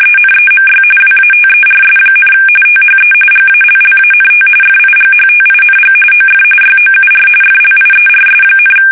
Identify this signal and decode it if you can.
RTTY, about 45.45 baud. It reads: ENEMY WATERS, THE GALLANT FIGHTING OF YOUR FLIERS,
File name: fsk850 / battleshipnj